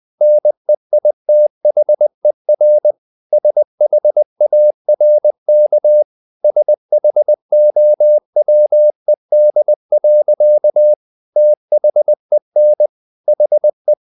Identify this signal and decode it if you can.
Morse code, heard as a keyed tone, about 20 wpm, 600 Hz. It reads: NEITHER SHARK SHOWED. THEN HE
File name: the old man and the sea part 2 20wpm